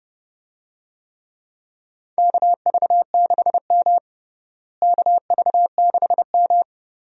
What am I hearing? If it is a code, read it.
Morse code, heard as a keyed tone, about 30 wpm, 700 Hz. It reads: KV6M KV6M
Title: USA Call Signs - 4 Characters - Repeated 2x - 30wpm